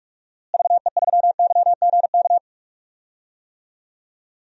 Morse code, heard as a keyed tone, about 46 wpm, 700 Hz. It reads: VE3YGK
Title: DXCC-100 Call Signs - 6 Characters - Repeated 3x - 45wpm